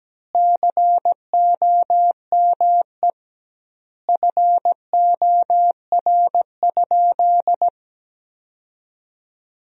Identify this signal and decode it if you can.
Morse code, heard as a keyed tone, about 17 wpm, 700 Hz. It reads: COME FOR?
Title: 2x Word-Spacing - Sentences from Top 100 Words - 17wpm